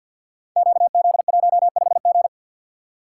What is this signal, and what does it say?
Morse code, heard as a keyed tone, about 47 wpm, 700 Hz. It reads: QZ15G
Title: Code Groups - 5 Letter:Numbers - 50wpm